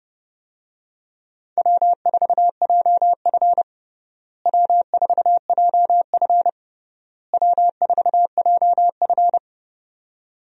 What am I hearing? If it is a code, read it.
Morse code, heard as a keyed tone, about 32 wpm, 700 Hz. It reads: W4JF W4JF W4JF
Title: USA Call Signs - 4 Characters - Repeated 3x - 30wpm